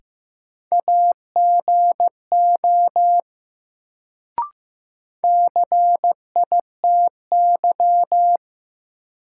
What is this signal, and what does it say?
Morse code, heard as a keyed tone, about 15 wpm, 700 Hz. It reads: AGO CITY
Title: Sets of 2 Words - New Word Drill - 15wpm